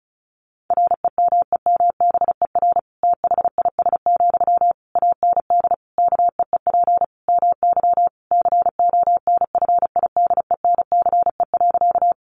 Morse code, heard as a keyed tone, about 34 wpm, 700 Hz. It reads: REMEMBER THIS, AND KEEP MY CONFIDENCE.
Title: Sentences from Top 1500 Words 35wpm